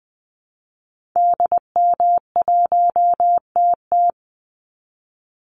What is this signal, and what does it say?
Morse code, heard as a keyed tone, about 20 wpm, 700 Hz. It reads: DM1TT
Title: Call Signs Repeated 3x - 20wpm